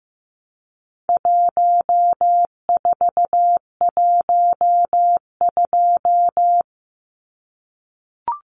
Morse code, heard as a keyed tone, about 15 wpm, 700 Hz. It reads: 1412
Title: Numbers - 4-Digit 15wpm